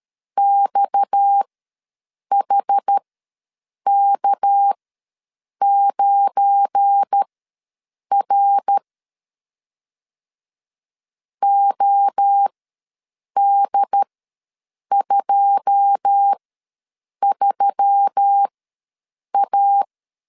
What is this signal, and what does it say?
Morse code, heard as a keyed tone, about 13 wpm, spaced as for 7 wpm, 800 Hz. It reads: XHK9R OD23A
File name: LEZIONI / 9 2 ws 12wpm 3406